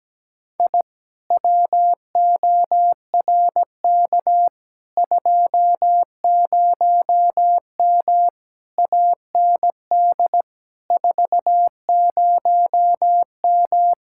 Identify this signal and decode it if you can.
Morse code, heard as a keyed tone, about 17 wpm, 700 Hz. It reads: I WORK 20M AND 40M
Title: Speed-Racing Common QSO Phrases Encore - 17wpm